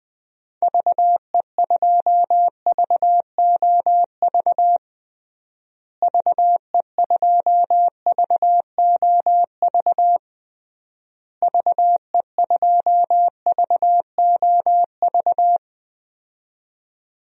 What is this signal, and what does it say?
Morse code, heard as a keyed tone, about 20 wpm, 700 Hz. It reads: VE2VOV VE2VOV VE2VOV
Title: DXCC-100 Call Signs - 6 Characters - Repeated 3x - 20wpm